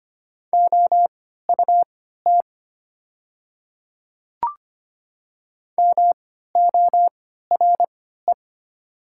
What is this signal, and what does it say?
Morse code, heard as a keyed tone, about 25 wpm, spaced as for 14 wpm, 700 Hz. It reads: OUT MORE
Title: 3x Character Spacing - Top 100 Words - 25wpm